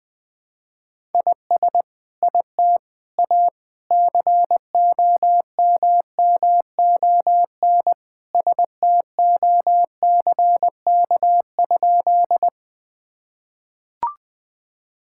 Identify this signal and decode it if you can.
Morse code, heard as a keyed tone, about 20 wpm, 700 Hz. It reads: IS IT A COMMON STOCK?
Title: Sentences from Top 1000 Words - Review Part 5 - 20wpm